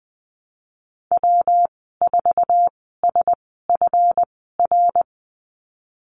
Morse code, heard as a keyed tone, about 20 wpm, spaced as for 14 wpm, 700 Hz. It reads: W4SFR
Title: 179 Lesson 29g - Slash - Call Signs 20wpm